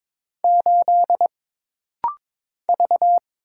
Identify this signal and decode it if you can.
Morse code, heard as a keyed tone, about 22 wpm, 700 Hz. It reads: 8 V
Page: This practice set is composed of Letters, Numbers, common punctuations, and common prosigns.